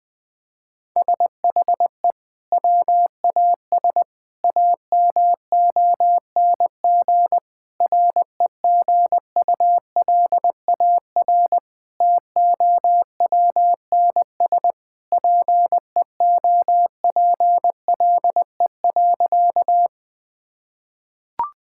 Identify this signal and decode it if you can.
Morse code, heard as a keyed tone, about 21 wpm, 700 Hz. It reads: SHE WAS AMONG REGULAR TOWNSPEOPLE.
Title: Sentences from Top 1000 Words - Review Part 6 - 20wpm